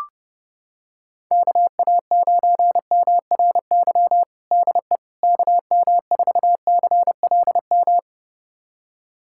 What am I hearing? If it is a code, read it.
Morse code, heard as a keyed tone, about 30 wpm, 700 Hz. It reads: KA9MRY DE KM4CLM
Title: USA Callsign DE Callsign - 6 Characters - 30wpm